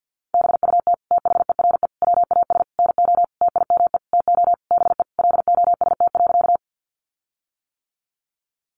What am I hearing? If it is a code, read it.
Morse code, heard as a keyed tone, about 50 wpm, 700 Hz. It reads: BUT THERE WAS NO TIME TO BE LOST.
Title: Sentences from Top 1400 Words 50wpm